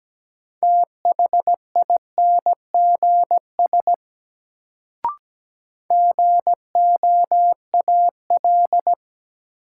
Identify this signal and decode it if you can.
Morse code, heard as a keyed tone, about 17 wpm, 700 Hz. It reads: THINGS GOAL
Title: Sentences from Top 1400 Words - New Word Drill Alt - 17wpm